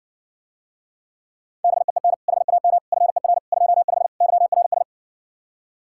Morse code, heard as a keyed tone, about 60 wpm, 700 Hz. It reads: BEEN HAM FER 25 YRS